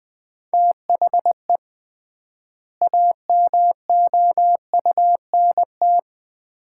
Morse code, heard as a keyed tone, about 20 wpm, 700 Hz. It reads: THE AMOUNT
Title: 3x Word-Spacing - Sets of 3 Words - 20wpm